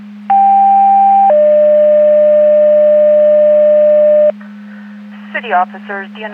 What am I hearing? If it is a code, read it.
Two-tone page, heard as a one-second tone, then a three-second tone, about 790 then 600 Hz, with speech. It is heard from 0.3 to 4.3 s.